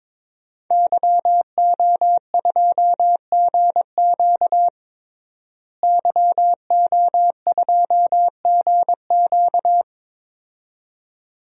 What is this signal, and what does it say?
Morse code, heard as a keyed tone, about 22 wpm, 700 Hz. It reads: YO2GQ YO2GQ
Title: DXCC-100 Call Signs - 5 Characters - Repeated 2x - 22wpm